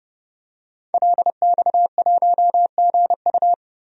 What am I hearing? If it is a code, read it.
Morse code, heard as a keyed tone, about 32 wpm, 700 Hz. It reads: LX1GU
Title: DXCC-100 Call Signs - 5 Characters - 30wpm